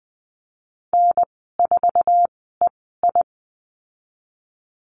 Morse code, heard as a keyed tone, about 19 wpm, spaced as for 14 wpm, 700 Hz. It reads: N4EI
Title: 048 Lesson 9g - Number 4 - Call Signs 20wpm